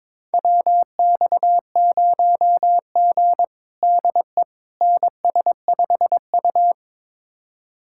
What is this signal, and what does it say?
Morse code, heard as a keyed tone, about 22 wpm, 700 Hz. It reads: WX0G DE NS5U
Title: USA Callsign DE Callsign - 4 Characters - 22wpm